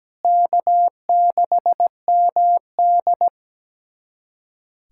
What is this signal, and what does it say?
Morse code, heard as a keyed tone, about 17 wpm, 700 Hz. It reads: K6MD
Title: USA Callsign DE Callsign - 4 Characters - 17wpm